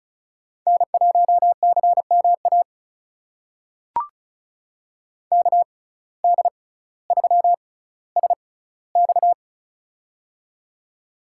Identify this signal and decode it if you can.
Morse code, heard as a keyed tone, about 34 wpm, spaced as for 18 wpm, 700 Hz. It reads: N1CMA KD3SX
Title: USA Call Signs - 5 Characters - 6x Character Spacing - 35wpm